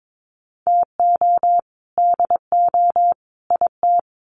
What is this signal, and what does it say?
Morse code, heard as a keyed tone, about 22 wpm, 700 Hz. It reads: TO DO IT
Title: Sets of 3 Words 22wpm